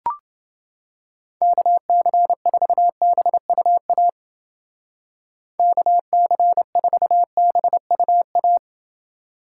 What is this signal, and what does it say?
Morse code, heard as a keyed tone, about 29 wpm, 700 Hz. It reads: KC4BUA KC4BUA